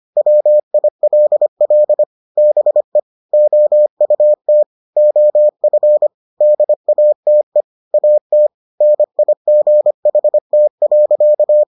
Morse code, heard as a keyed tone, about 25 wpm, 600 Hz. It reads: WILL BE OUT OF DATE AT NIGHT.